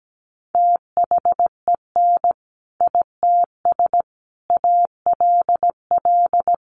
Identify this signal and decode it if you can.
Morse code, heard as a keyed tone, about 17 wpm, 700 Hz. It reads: THEN ITS ALL
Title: Sentences from Top 600 Words 17wpm